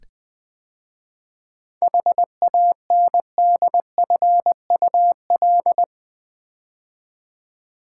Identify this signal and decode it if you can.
Morse code, heard as a keyed tone, about 20 wpm, 700 Hz. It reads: HANDFUL
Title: 7 Letter Words - First 3 Letters Given - 20wpm